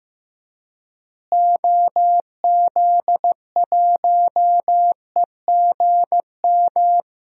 Morse code, heard as a keyed tone, about 15 wpm, 700 Hz. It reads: OZ1EGM
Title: DXCC-100 Call Signs - 6 Characters - Repeated 3x - 15wpm